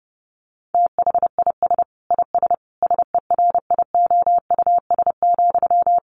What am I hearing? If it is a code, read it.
Morse code, heard as a keyed tone, about 30 wpm, 700 Hz. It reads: THIS IS SERIOUS,
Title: Sentences from Top 1100 Words 30wpm